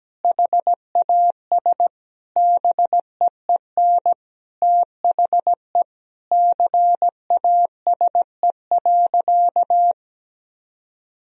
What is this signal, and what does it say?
Morse code, heard as a keyed tone, about 17 wpm, 700 Hz. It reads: HAS BEEN THE CASE.
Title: Sentences from Top 1000 Words - Review Part 3 - 17wpm